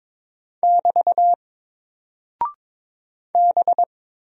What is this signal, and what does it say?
Morse code, heard as a keyed tone, about 22 wpm, 700 Hz. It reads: <BT> B
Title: Single Character (Letter:Number:Punctuation:Prosign) Alt 22wpm